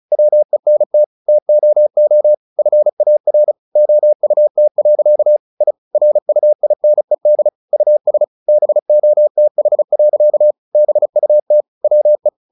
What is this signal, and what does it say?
Morse code, heard as a keyed tone, about 34 wpm, 600 Hz. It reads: WENT TOO FAR OUT. I RUINED US BOTH. BUT WE
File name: the old man and the sea 2 35wpm